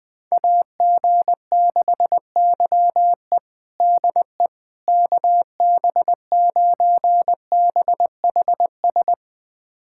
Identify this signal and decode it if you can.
Morse code, heard as a keyed tone, about 21 wpm, 700 Hz. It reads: AG6YE DE KB9BHS
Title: USA Callsign DE Callsign 20wpm